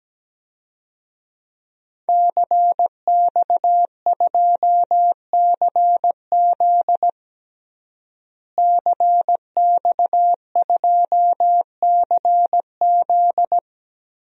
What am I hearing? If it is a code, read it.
Morse code, heard as a keyed tone, about 17 wpm, 700 Hz. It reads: CX2CZ CX2CZ
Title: DXCC-100 Call Signs - 5 Characters - Repeated 3x - 17wpm